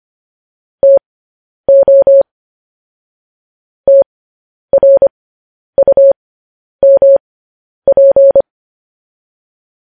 Morse code, heard as a keyed tone, about 25 wpm, spaced as for 10 wpm, 550 Hz. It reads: TO TRUMP
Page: news headlines in morse code at 10 wpm